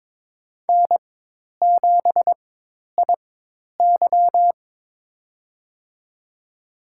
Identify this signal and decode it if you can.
Morse code, heard as a keyed tone, about 22 wpm, spaced as for 10 wpm, 700 Hz. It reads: N7IY
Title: USA Call Signs - 4 Characters - 4x Character Spacing - 22wpm